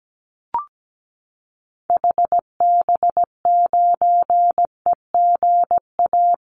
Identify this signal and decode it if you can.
Morse code, heard as a keyed tone, about 17 wpm, 700 Hz. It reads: HB9EGA
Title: Call Signs - 17wpm